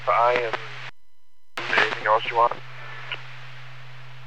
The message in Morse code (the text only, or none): none